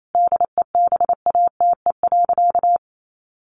Morse code, heard as a keyed tone, about 28 wpm, 700 Hz. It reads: DEBATE.
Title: Sentences from Top 1100 Words 28wpm